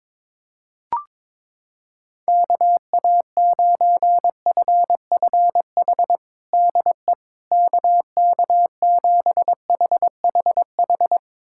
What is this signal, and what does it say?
Morse code, heard as a keyed tone, about 22 wpm, 700 Hz. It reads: KA9FFH DE KK7HHH
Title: USA Callsign DE Callsign - 6 Characters - 22wpm